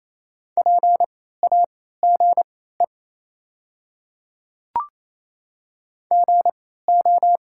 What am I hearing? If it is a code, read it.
Morse code, heard as a keyed tone, about 28 wpm, spaced as for 16 wpm, 700 Hz. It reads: PAGE GO
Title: 3x Character Spacing - Top 300 Words - 28wpm